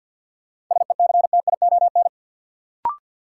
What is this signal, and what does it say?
Morse code, heard as a keyed tone, about 48 wpm, 700 Hz. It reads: SECTION
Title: Sentences from Top 1000 Words - Review Part 5 - New Word Drill - 50wpm